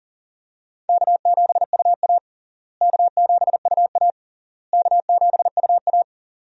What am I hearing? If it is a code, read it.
Morse code, heard as a keyed tone, about 37 wpm, 700 Hz. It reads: K7UA K7UA K7UA
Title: USA Call Signs - 4 Characters - Repeated 3x - 40wpm